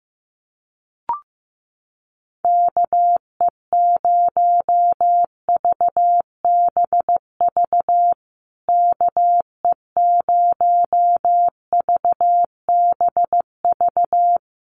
Morse code, heard as a keyed tone, about 15 wpm, 700 Hz. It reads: KE0VBV KE0VBV
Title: USA Call Signs Repeated 3x 15wpm